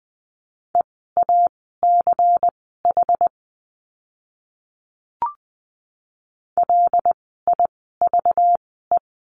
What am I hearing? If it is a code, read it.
Morse code, heard as a keyed tone, about 20 wpm, spaced as for 15 wpm, 700 Hz. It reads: EACH LIVE
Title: 192 Lesson 31f - Letter v - Words 20wpm